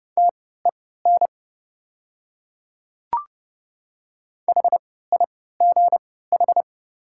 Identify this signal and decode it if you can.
Morse code, heard as a keyed tone, about 30 wpm, spaced as for 17 wpm, 690 Hz. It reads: TEN HIGH
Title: 3x Character Spacing - Top 500 Words - 30wpm